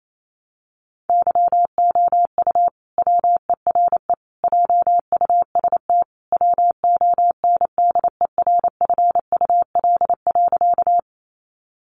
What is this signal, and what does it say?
Morse code, heard as a keyed tone, about 28 wpm, 700 Hz. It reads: YOU WERE JUST WONDERFUL.
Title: Sentences from Top 1100 Words 28wpm